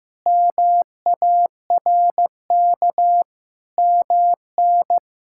Morse code, heard as a keyed tone, about 15 wpm, 700 Hz. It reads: MARK MN
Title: K1USN SST QSO Practice - 15wpm